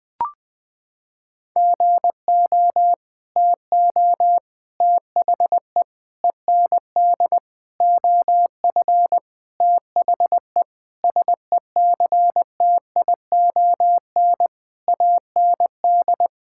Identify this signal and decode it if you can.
Morse code, heard as a keyed tone, about 19 wpm, 700 Hz. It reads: GO TO THE END OF THE SECTION AND
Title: Sentences from Top 1000 Words - Review Part 5 - 20wpm